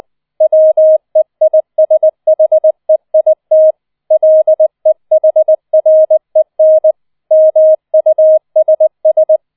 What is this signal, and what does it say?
Morse code, heard as a keyed tone, about 20 wpm, 630 Hz. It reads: WEISHEIT LEHREN MUSS
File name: 004 Vierter Streich 20wpm